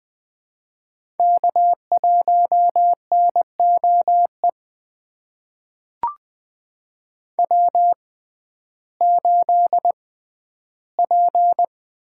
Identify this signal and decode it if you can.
Morse code, heard as a keyed tone, about 21 wpm, spaced as for 12 wpm, 700 Hz. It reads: K1NOE W8P
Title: USA Call Signs - 5 Characters - 6x Character Spacing - 20wpm